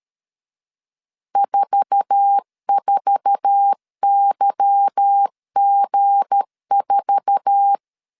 Morse code, heard as a keyed tone, about 13 wpm, 800 Hz. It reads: 44YG4
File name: LEZIONI / 8 YJ45 / 8 YJ4 12wpm 2151